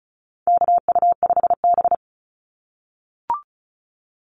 Morse code, heard as a keyed tone, about 34 wpm, 700 Hz. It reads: KU5B
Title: Call Signs - 35wpm